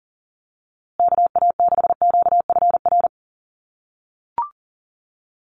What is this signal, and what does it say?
Morse code, heard as a keyed tone, about 40 wpm, 700 Hz. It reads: KA6QFR
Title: USA Call Signs Repeated 3x 40wpm